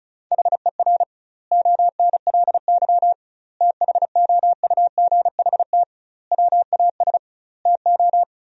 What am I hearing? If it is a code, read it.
Morse code, heard as a keyed tone, about 34 wpm, 700 Hz. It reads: HER ONLY THOUGHT WAS TO
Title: 2x Word-Spacing - Sentences from Top 400 Words - 35wpm